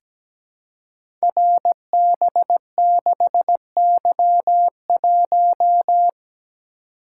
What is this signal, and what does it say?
Morse code, heard as a keyed tone, about 17 wpm, 690 Hz. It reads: RB6Y1
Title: Code Groups - 5 Letter:Numbers - 17wpm